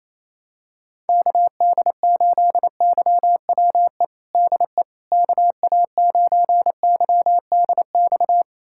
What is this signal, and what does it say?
Morse code, heard as a keyed tone, about 28 wpm, 700 Hz. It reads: KD8YWE DE KA9YDX
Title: USA Callsign DE Callsign 28wpm